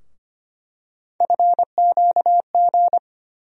Morse code, heard as a keyed tone, about 25 wpm, 700 Hz. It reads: FQG
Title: USA Call Sign Postfixes - 3 Letters 25wpm